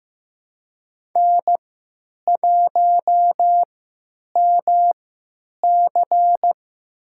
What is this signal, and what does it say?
Morse code, heard as a keyed tone, about 15 wpm, spaced as for 9 wpm, 700 Hz. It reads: N1MC
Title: USA Call Signs - 4 Characters - 3x Character Spacing - 15wpm